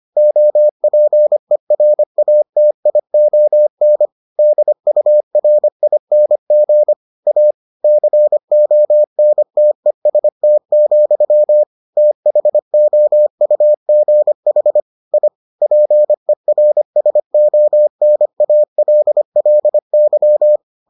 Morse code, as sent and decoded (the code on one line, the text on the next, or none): --- .--. .-. .- - .. --- -. / -.. ..- .-. .. -. --. / .- / -.-. --- -. - . ... - --..-- / - .... --- ..- --. .... / .. / .--. .-. ... --- -. .- .-.. .-.. -.--
OPERATION DURING A CONTEST, THOUGH I PERSONALLY